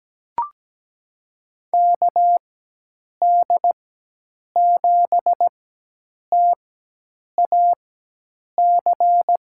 Morse code, heard as a keyed tone, about 17 wpm, spaced as for 8 wpm, 700 Hz. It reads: KD7TAC
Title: USA Call Signs - 6 Characters - 4x Character Spacing - 17wpm